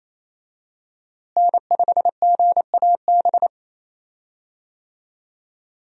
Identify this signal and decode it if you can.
Morse code, heard as a keyed tone, about 28 wpm, 700 Hz. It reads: N5GAB
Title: USA Call Signs - 5 Characters - Repeated 2x - 28wpm